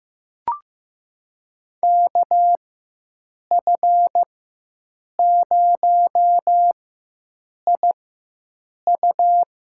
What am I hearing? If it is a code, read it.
Morse code, heard as a keyed tone, about 15 wpm, spaced as for 7 wpm, 700 Hz. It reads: KF0IU
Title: USA Call Signs - 4x Character Spacing - 15wpm